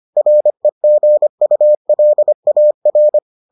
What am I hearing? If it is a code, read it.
Morse code, heard as a keyed tone, about 25 wpm, 600 Hz. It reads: REGULAR